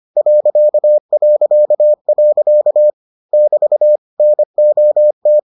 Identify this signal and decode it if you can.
Morse code, heard as a keyed tone, about 25 wpm, 600 Hz. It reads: ... <BT> NOT